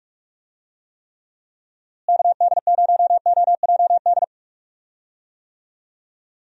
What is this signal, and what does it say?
Morse code, heard as a keyed tone, about 46 wpm, 700 Hz. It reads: KD0OJD
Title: USA Call Signs - 6 Characters - Repeated 2x - 45wpm